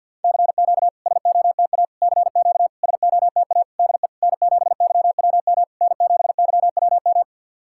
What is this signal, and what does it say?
Morse code, heard as a keyed tone, about 50 wpm, 690 Hz. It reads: CQ SOTA CQ SOTA DE N7YWM N7YWM
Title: SOTA QSO Practice 50wpm